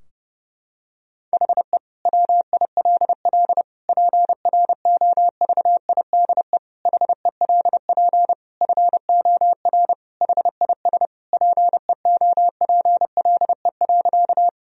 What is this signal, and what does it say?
Morse code, heard as a keyed tone, about 32 wpm, 700 Hz. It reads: HE WILL PROVIDE HELP FOR HIS PEOPLE.
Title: Sentences from Top 1000 Words - Review Part 3 - 30wpm